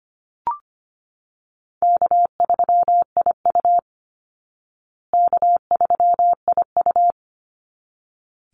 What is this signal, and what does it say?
Morse code, heard as a keyed tone, about 25 wpm, 700 Hz. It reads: K3IU K3IU
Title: Easy Call Signs Repeated 2x - 25wpm